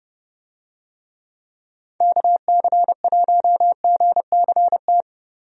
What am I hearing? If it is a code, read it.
Morse code, heard as a keyed tone, about 30 wpm, 700 Hz. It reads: KC1GCT
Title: USA Call Signs - 6 Characters - Repeated 3x - 30wpm